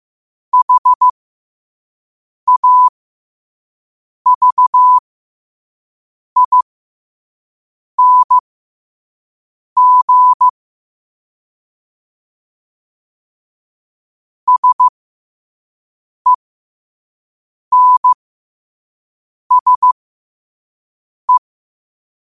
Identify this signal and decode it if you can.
Morse code, heard as a keyed tone, about 15 wpm, spaced as for 5 wpm, 1000 Hz. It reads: HAVING SENSE
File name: qotdfiles / 5wpm